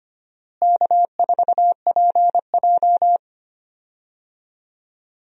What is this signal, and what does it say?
Morse code, heard as a keyed tone, about 25 wpm, 700 Hz. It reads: K4PJ
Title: USA Call Signs - 4 Characters - Repeated 3x - 25wpm